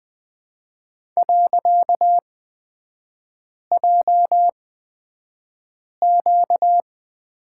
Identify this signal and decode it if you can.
Morse code, heard as a keyed tone, about 20 wpm, spaced as for 5 wpm, 700 Hz. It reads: .JQ
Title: 255 Lesson 41e - Period - Three Characters 20wpm